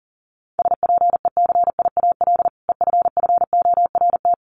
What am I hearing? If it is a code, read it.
Morse code, heard as a keyed tone, about 40 wpm, 700 Hz. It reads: SPECIAL EFFORT
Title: Sentences from Top 1200 Words 40wpm